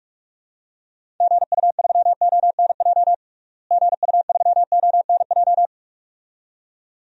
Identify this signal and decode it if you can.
Morse code, heard as a keyed tone, about 42 wpm, 700 Hz. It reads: GU3ONJ GU3ONJ
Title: DXCC-100 Call Signs - 6 Characters - Repeated 2x - 45wpm